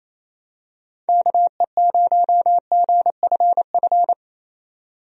Morse code, heard as a keyed tone, about 27 wpm, 700 Hz. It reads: KE0GFF
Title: USA Call Signs - 6 Characters - 28wpm